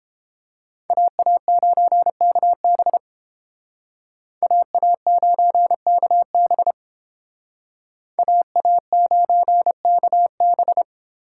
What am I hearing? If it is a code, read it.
Morse code, heard as a keyed tone, about 30 wpm, 700 Hz. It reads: AA9KB AA9KB AA9KB